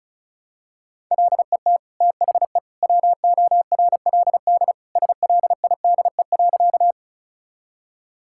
Morse code, heard as a keyed tone, about 34 wpm, 700 Hz. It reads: LET THE WORLD SLIDE.
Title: Sentences from Top 1600 Words 35wpm